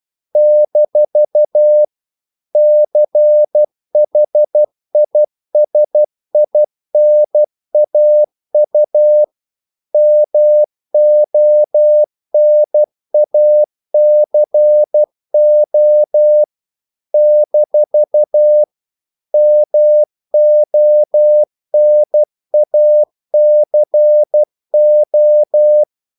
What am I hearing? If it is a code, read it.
Morse code, heard as a keyed tone, about 12 wpm, 600 Hz. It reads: - CHISINAU MONACO - MONACO
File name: CW / Practice / NationalCapitals12